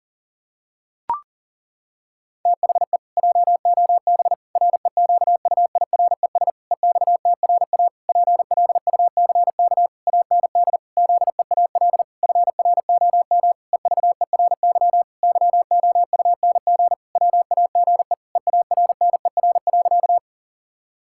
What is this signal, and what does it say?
Morse code, heard as a keyed tone, about 40 wpm, 700 Hz. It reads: THE JOB REQUIRES EXTRA PLUCK AND ZEAL FROM EVERY YOUNG WAGE EARNER.
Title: Pangram Challenge v2 - 40wpm